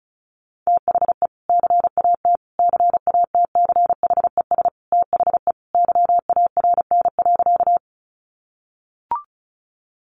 Morse code, heard as a keyed tone, about 34 wpm, 700 Hz. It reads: THE CAT CATCHES THE YARN.
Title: Additional Easy Phrases 35wpm